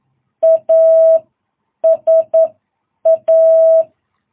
Morse code, heard as a keyed tone, about 11 wpm, spaced as for 8 wpm, 660 Hz. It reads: ASA
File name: jh8cbh / 2022cw